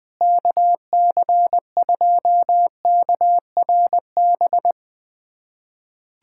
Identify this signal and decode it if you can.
Morse code, heard as a keyed tone, about 21 wpm, 700 Hz. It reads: KC2KRB
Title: USA Callsign DE Callsign - 6 Characters - 20wpm